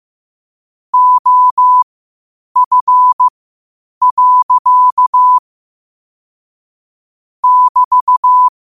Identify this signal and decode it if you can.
Morse code, heard as a keyed tone, about 15 wpm, spaced as for 8 wpm, 1000 Hz. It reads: OF. <BT>